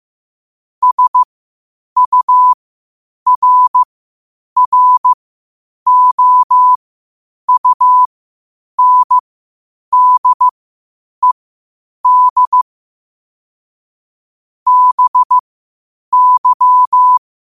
Morse code, heard as a keyed tone, about 15 wpm, spaced as for 8 wpm, 1000 Hz. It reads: SURROUNDED BY
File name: qotdfiles / 8wpm